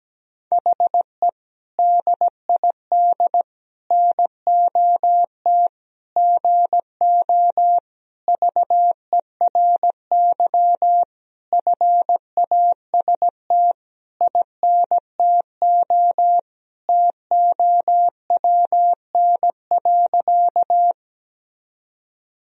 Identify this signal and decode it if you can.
Morse code, heard as a keyed tone, about 17 wpm, 700 Hz. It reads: HE DID NOT GO VERY FAST INTO TOWN.
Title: Sentences from Top 1000 Words - Review Part 2 - 17wpm